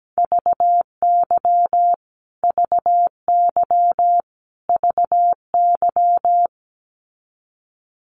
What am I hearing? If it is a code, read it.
Morse code, heard as a keyed tone, about 17 wpm, 700 Hz. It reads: VY VY VY
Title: Abbreviations Repeated 3x - 17wpm